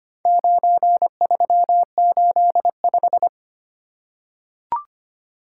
Morse code, heard as a keyed tone, about 25 wpm, 700 Hz. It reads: 9385
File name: practice / 25wpm / Four Letter-Number Combinations - 25wpm